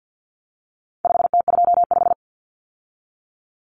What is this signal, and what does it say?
Morse code, heard as a keyed tone, about 50 wpm, 700 Hz. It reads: 5T25
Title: Common Cut Numbers - 4-Digit 50wpm